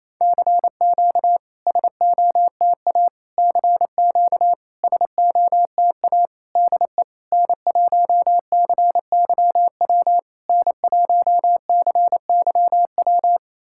Morse code, heard as a keyed tone, about 27 wpm, 690 Hz. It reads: CQ SOTA CQ SOTA DE N1CYW N1CYW
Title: SOTA QSO Practice 28wpm